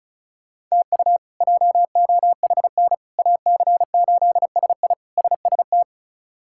Morse code, heard as a keyed tone, about 36 wpm, 700 Hz. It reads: TU JOHN AC8SI SST
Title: K1USN SST QSO Practice - 35wpm